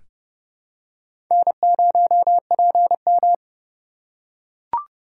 Morse code, heard as a keyed tone, about 30 wpm, 700 Hz. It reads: N0PM
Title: USA Call Signs - 4 Characters - 4x Character Spacing - 30wpm